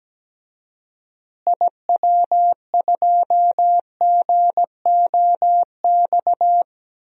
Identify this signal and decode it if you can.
Morse code, heard as a keyed tone, about 17 wpm, 700 Hz. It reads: IW2GOX
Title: DXCC-100 Call Signs - 6 Characters - Repeated 2x - 17wpm